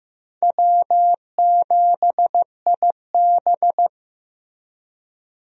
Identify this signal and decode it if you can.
Morse code, heard as a keyed tone, about 16 wpm, 700 Hz. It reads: W7IB
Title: USA Call Signs - 4 Characters - Repeated 2x - 15wpm